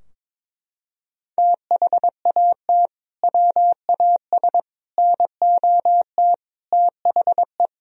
Morse code, heard as a keyed tone, about 22 wpm, 700 Hz. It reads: THAT WAS NOT THE
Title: Sentences from Top 1000 Words - Review Part 2 - 22wpm